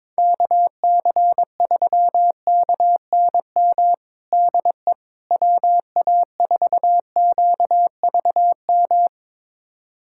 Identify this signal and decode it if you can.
Morse code, heard as a keyed tone, about 22 wpm, 700 Hz. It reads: KC3KNM DE WA4QVM
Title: USA Callsign DE Callsign - 6 Characters - 22wpm